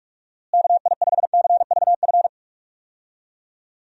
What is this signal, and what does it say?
Morse code, heard as a keyed tone, about 45 wpm, 700 Hz. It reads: KI5CVF
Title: USA Call Signs - 6 Characters - Repeated 2x - 45wpm